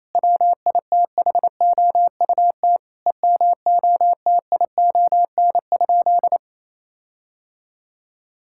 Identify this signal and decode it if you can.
Morse code, heard as a keyed tone, about 28 wpm, 700 Hz. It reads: WITHOUT EMOTION?
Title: Sentences from Top 1600 Words 28wpm